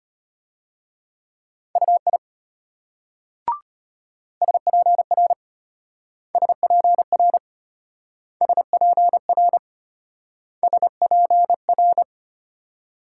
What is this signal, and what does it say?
Morse code, heard as a keyed tone, about 33 wpm, 700 Hz. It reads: UI SPR SPR SPR SPR